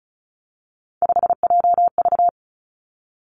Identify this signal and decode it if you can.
Morse code, heard as a keyed tone, about 36 wpm, 700 Hz. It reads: 5JV
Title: Three Challenging Letter-Number Combinations 35wpm